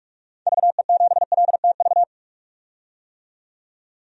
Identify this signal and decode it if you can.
Morse code, heard as a keyed tone, about 46 wpm, 700 Hz. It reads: VE7LTV
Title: DXCC-100 Call Signs - 6 Characters - 45wpm